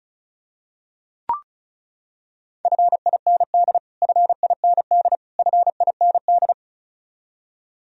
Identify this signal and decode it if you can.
Morse code, heard as a keyed tone, about 36 wpm, 700 Hz. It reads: FIND FIND FIND
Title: Top 100 Words Repeated 35wpm